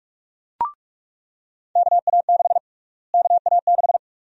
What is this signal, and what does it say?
Morse code, heard as a keyed tone, about 46 wpm, 700 Hz. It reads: KA6 KA6
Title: USA Call Sign Prefixes - Repeated 2x - 45wpm